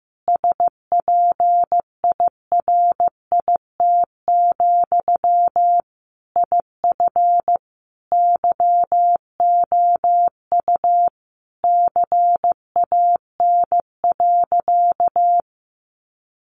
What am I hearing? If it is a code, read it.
Morse code, heard as a keyed tone, about 15 wpm, 700 Hz. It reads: SPIRIT, IF YOU CAN.
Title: Sentences from Top 1400 Words 15wpm